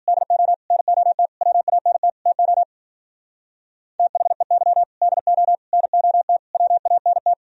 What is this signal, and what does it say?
Morse code, heard as a keyed tone, about 50 wpm, 700 Hz. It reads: DO NOT WANT TO THEY DO NOT WANT
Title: Speed-Racing - Sets of 5 Words 40wpm